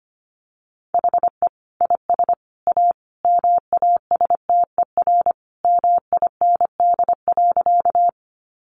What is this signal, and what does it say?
Morse code, heard as a keyed tone, about 25 wpm, 700 Hz. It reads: HE IS A MASTER MIND.
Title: Sentences from Top 1300 Words 25wpm